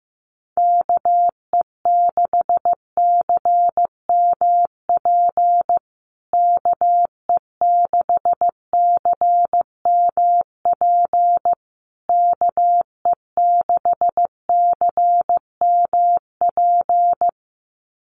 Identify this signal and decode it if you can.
Morse code, heard as a keyed tone, about 15 wpm, 700 Hz. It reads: KE6CMP KE6CMP KE6CMP
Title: USA Call Signs Repeated 3x 15wpm